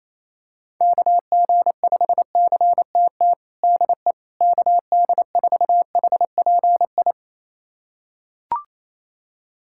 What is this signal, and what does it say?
Morse code, heard as a keyed tone, about 28 wpm, 700 Hz. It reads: KG5CTT DE KD4HPI
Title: USA Callsign DE Callsign 28wpm